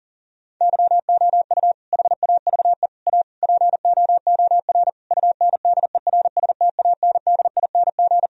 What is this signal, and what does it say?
Morse code, heard as a keyed tone, about 40 wpm, 700 Hz. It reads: YOU HAVE A POOR UNDERSTANDING
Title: Sentences from Top 1000 Words - Review Part 3 - 40wpm